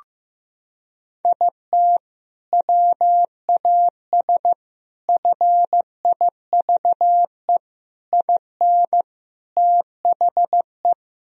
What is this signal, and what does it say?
Morse code, heard as a keyed tone, about 15 wpm, 700 Hz. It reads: IT WAS FIVE IN THE
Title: Sentences from Top 1000 Words - Review Part 2 - 15wpm